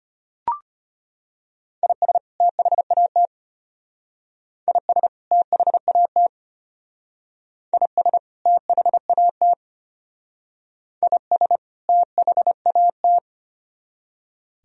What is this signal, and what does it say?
Morse code, heard as a keyed tone, about 33 wpm, 700 Hz. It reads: IS THAT IS THAT IS THAT IS THAT